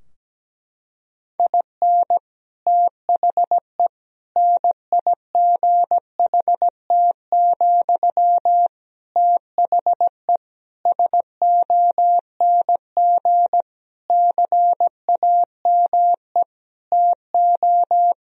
Morse code, heard as a keyed tone, about 17 wpm, 700 Hz. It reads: IN THE NIGHT, THE SONG CAME TO
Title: Sentences from Top 1000 Words - Review Part 2 - 17wpm